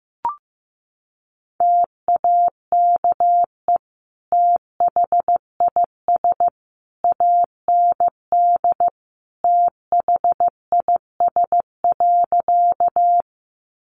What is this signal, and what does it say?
Morse code, heard as a keyed tone, about 15 wpm, 700 Hz. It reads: TAKE THIS AND THIS.
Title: Sentences from Top 200 Words 15wpm